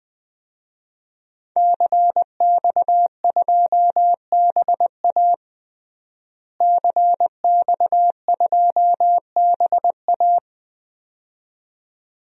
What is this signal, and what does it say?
Morse code, heard as a keyed tone, about 20 wpm, 700 Hz. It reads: CX2BA CX2BA
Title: DXCC-100 Call Signs - 5 Characters - Repeated 2x - 20wpm